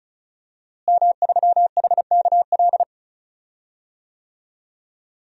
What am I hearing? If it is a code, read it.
Morse code, heard as a keyed tone, about 35 wpm, 700 Hz. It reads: M3HKL
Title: DXCC-100 Call Signs - 5 Characters - Repeated 2x - 35wpm